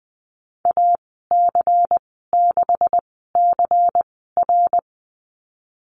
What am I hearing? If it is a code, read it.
Morse code, heard as a keyed tone, about 20 wpm, spaced as for 14 wpm, 700 Hz. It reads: AC6CR
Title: 154 Lesson 25g - Letter p - Call Signs 20wpm